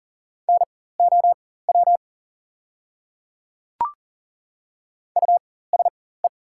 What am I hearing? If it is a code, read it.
Morse code, heard as a keyed tone, about 39 wpm, spaced as for 19 wpm, 700 Hz. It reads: NOW USE
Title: 4x Character Spacing - Top 100 Words - 40wpm